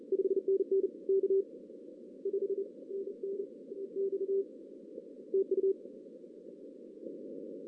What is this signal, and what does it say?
Morse code, heard as a keyed tone, about 33 wpm, 380 Hz. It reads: 5NN K 5 NN X TU
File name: 2016 arrl dx cw / 2016-ARRL-DX-CW-K5TR- - 0-3